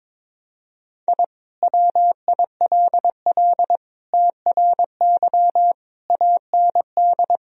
Morse code, heard as a keyed tone, about 21 wpm, 700 Hz. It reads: I WILL TRY AND
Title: Sentences from Top 1000 Words - Review Part 3 - 22wpm